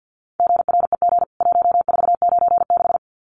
Morse code, heard as a keyed tone, about 50 wpm, 700 Hz. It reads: GREG 1496
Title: CWT Exchange 50wpm